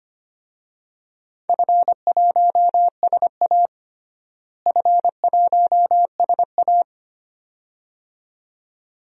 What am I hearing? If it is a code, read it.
Morse code, heard as a keyed tone, about 25 wpm, 700 Hz. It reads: F1SA F1SA
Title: DXCC-100 Call Signs - 4 Characters - Repeated 2x - 25wpm